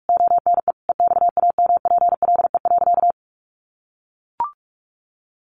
Morse code, heard as a keyed tone, about 44 wpm, 700 Hz. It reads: ONE EXAMPLE.
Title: Sentences from Top 1500 Words 45wpm